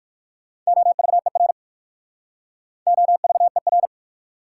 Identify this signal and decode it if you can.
Morse code, heard as a keyed tone, about 60 wpm, 700 Hz. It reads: OVER OVER